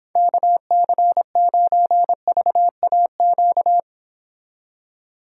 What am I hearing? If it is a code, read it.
Morse code, heard as a keyed tone, about 31 wpm, 700 Hz. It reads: KC9VAQ